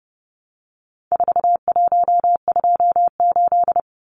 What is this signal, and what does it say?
Morse code, heard as a keyed tone, about 30 wpm, 700 Hz. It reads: 4128
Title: Numbers - 4-Digit 30wpm